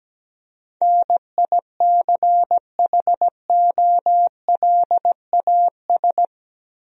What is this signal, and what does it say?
Morse code, heard as a keyed tone, about 17 wpm, 700 Hz. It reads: NICHOLAS
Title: USA Common Names - 17wpm